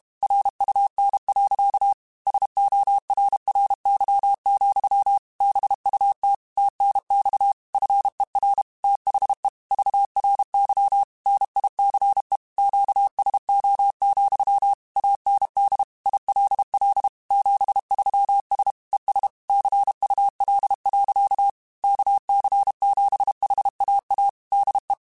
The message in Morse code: RUN. SORRY, BUT TNX FER THE VRY NICE QSO, AND ILL 73S ES CUL. KC7HAA DE

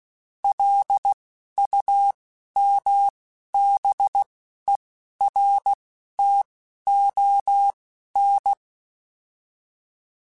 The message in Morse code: LUMBERTON

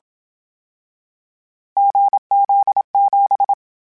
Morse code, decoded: GZ7

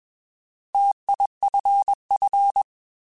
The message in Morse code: TIFF

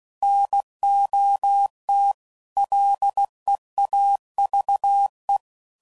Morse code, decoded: NOT LEAVE

